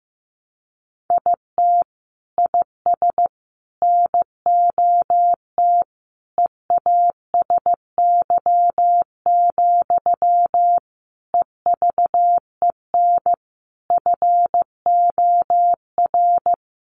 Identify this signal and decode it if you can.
Morse code, heard as a keyed tone, about 15 wpm, 700 Hz. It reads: IT IS NOT EASY, EVEN FOR
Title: Sentences from Top 700 Words 15wpm